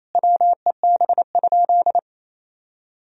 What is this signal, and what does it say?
Morse code, heard as a keyed tone, about 28 wpm, 700 Hz. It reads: WEB?
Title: Sentences from Top 1000 Words - Review Part 4 - 28wpm